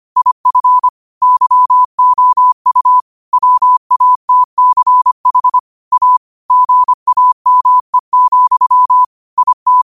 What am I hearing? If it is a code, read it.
Morse code, heard as a keyed tone, about 25 wpm, 1000 Hz. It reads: IF YOU WATCH A GAME, IT